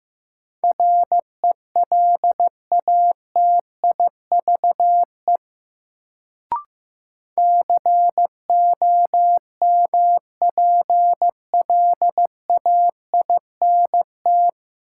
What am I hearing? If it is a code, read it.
Morse code, heard as a keyed tone, about 15 wpm, 700 Hz. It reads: RELATIVE COMPLAINT
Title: Sentences from Top 1400 Words - New Word Drill - 15wpm